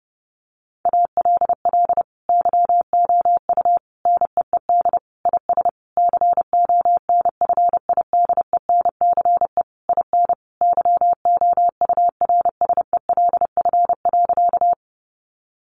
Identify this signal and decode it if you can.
Morse code, heard as a keyed tone, about 30 wpm, 700 Hz. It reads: ALL YOU NEED IS CONFIDENCE IN YOURSELF.
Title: Sentences from Top 1500 Words 30wpm